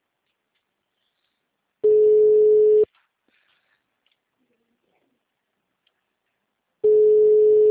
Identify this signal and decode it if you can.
Ringback tone (Europe, 425 Hz).
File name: out- -2019- - - .1599